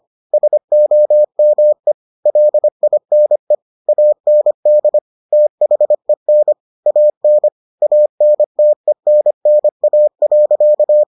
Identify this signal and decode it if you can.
Morse code, heard as a keyed tone, about 25 wpm, 600 Hz. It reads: SOME LINE AND THEN AN ANTENNA.